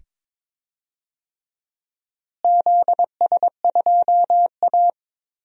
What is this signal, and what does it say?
Morse code, heard as a keyed tone, about 22 wpm, 700 Hz. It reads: ZS2A
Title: DXCC-100 Call Signs - 4 Characters - Repeated 2x - 22wpm